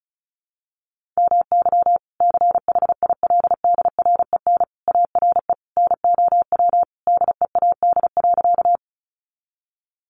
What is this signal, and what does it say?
Morse code, heard as a keyed tone, about 34 wpm, 700 Hz. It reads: MY CHILDREN ARE NOW DEAD.
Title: Sentences from Top 1500 Words 35wpm